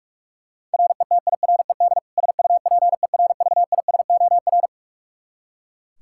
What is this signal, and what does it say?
Morse code, heard as a keyed tone, about 43 wpm, 700 Hz. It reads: RETIRED SUPERVISOR
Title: Practice QSO - Exchange 3 - More Info - 45wpm